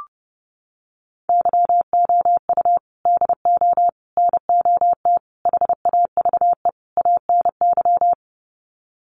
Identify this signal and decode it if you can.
Morse code, heard as a keyed tone, about 30 wpm, 700 Hz. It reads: YOU DO NOT HAVE ANY
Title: Sets of 5 Words Encore 30wpm